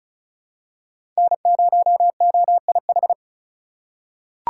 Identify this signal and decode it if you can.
Morse code, heard as a keyed tone, about 36 wpm, 700 Hz. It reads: N0OIH
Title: USA Call Signs - 5 Characters - 3x Character Spacing - 35wpm